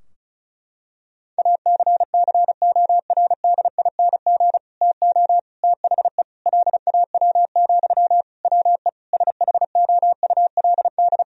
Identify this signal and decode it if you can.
Morse code, heard as a keyed tone, about 36 wpm, 700 Hz. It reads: ACCORDING TO THE LAW, WE SHOULD
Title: Sentences from Top 1000 Words - Review Part 4 - 35wpm